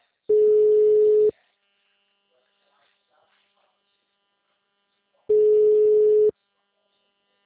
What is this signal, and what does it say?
Ringback tone (Europe, 425 Hz), with speech.